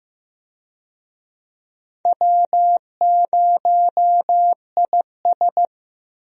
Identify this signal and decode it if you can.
Morse code, heard as a keyed tone, about 15 wpm, 700 Hz. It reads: W0IS
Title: USA Call Signs - 4 Characters - Repeated 3x - 15wpm